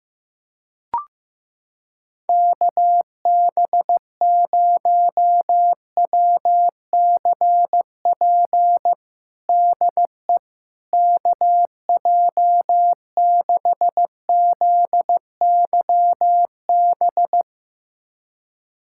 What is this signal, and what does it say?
Morse code, heard as a keyed tone, about 15 wpm, 700 Hz. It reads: KB0WCP DE KJ6ZYB
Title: USA Callsign DE Callsign - 6 Characters - 15wpm